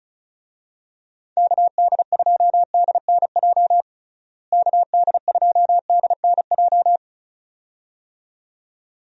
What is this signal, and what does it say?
Morse code, heard as a keyed tone, about 36 wpm, 700 Hz. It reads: KD2DNJ KD2DNJ
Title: USA Call Signs - 6 Characters - Repeated 2x - 35wpm